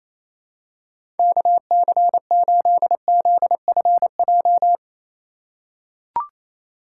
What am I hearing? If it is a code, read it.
Morse code, heard as a keyed tone, about 28 wpm, 690 Hz. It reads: KC8ZFJ
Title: USA Call Signs - 2x Character Spacing - 28wpm